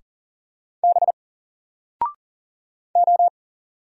Morse code, heard as a keyed tone, about 40 wpm, 700 Hz. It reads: B O
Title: Single Letter-Number Alt 40wpm